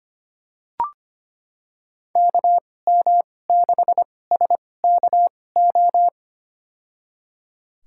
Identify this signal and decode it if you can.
Morse code, heard as a keyed tone, about 25 wpm, spaced as for 18 wpm, 700 Hz. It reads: KM6SKO
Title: USA Call Signs - 6 Characters - 2x Character Spacing - 25wpm